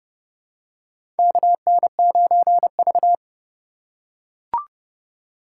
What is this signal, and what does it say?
Morse code, heard as a keyed tone, about 30 wpm, 700 Hz. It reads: KN9V
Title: USA Call Signs - 4 Characters - 2x Character Spacing - 30wpm